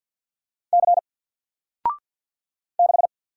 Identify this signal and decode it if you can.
Morse code, heard as a keyed tone, about 48 wpm, 700 Hz. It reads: C 6
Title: Single Letter-Number 50wpm